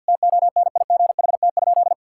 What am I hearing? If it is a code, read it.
Morse code, heard as a keyed tone, about 52 wpm, 700 Hz. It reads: TONIGHT?
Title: Sentences from Top 1000 Words - Review Part 5 - 50wpm